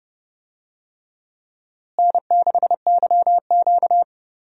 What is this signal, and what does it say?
Morse code, heard as a keyed tone, about 30 wpm, 700 Hz. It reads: N6YQ
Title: USA Call Signs - 4 Characters - Repeated 3x - 30wpm